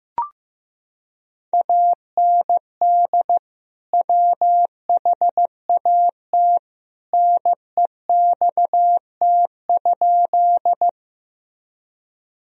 Morse code, heard as a keyed tone, about 15 wpm, 700 Hz. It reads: AND WHAT NEXT?
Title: Sentences from Top 1000 Words - Review Part 1 - 15wpm